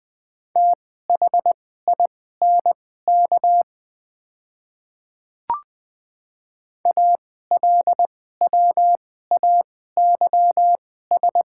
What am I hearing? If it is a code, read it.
Morse code, heard as a keyed tone, about 20 wpm, spaced as for 14 wpm, 700 Hz. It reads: THINK ALWAYS
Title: 2x Character Spacing - Top 300 Words - 20wpm